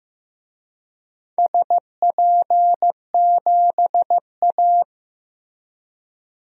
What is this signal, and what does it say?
Morse code, heard as a keyed tone, about 15 wpm, 700 Hz. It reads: SP7A
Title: DXCC-100 Call Signs - 4 Characters - Repeated 2x - 15wpm